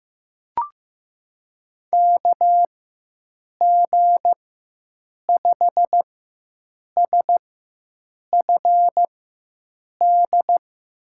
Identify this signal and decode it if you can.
Morse code, heard as a keyed tone, about 15 wpm, spaced as for 7 wpm, 700 Hz. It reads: KG5SFD
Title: USA Call Signs - 6 Characters - 4x Character Spacing - 15wpm